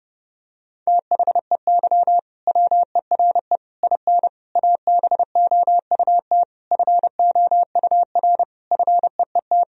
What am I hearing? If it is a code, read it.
Morse code, heard as a keyed tone, about 30 wpm, 700 Hz. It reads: THEY WERE IN ABOUT FOUR FEET
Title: Sentences from Top 1000 Words - Review Part 1 - 30wpm